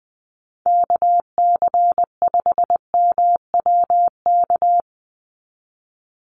Morse code, heard as a keyed tone, about 20 wpm, 700 Hz. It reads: KC5MWK
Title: USA Call Signs 20wpm